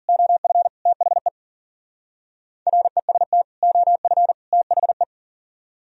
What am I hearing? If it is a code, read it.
Morse code, heard as a keyed tone, about 48 wpm, 700 Hz. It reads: OF THE REST OF THE
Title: Speed-Racing - Sets of 3 Words 40wpm